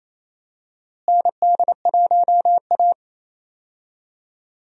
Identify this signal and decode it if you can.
Morse code, heard as a keyed tone, about 28 wpm, 700 Hz. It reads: ND1A
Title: USA Call Signs - 4 Characters - Repeated 3x - 28wpm